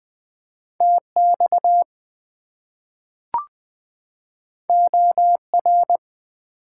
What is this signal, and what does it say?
Morse code, heard as a keyed tone, about 20 wpm, 700 Hz. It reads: TX OR
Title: US States and Canadian Provinces 20wpm